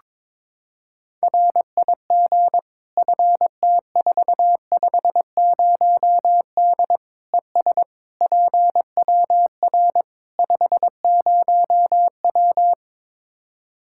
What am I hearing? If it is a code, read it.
Morse code, heard as a keyed tone, about 22 wpm, 700 Hz. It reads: RIG FT450D ES PWR 50W
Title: K1USN SST QSO Practice 22wpm